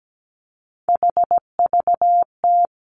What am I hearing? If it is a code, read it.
Morse code, heard as a keyed tone, about 17 wpm, 700 Hz. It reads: HVT
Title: Three Challenging Letter-Number Combinations 17wpm